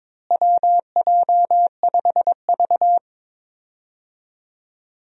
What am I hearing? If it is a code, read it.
Morse code, heard as a keyed tone, about 22 wpm, 700 Hz. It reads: WJ5V
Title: USA Call Signs - 4 Characters - Repeated 2x - 22wpm